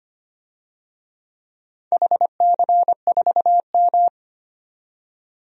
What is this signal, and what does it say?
Morse code, heard as a keyed tone, about 25 wpm, 700 Hz. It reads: HC4M
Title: DXCC-100 Call Signs - 4 Characters - 25wpm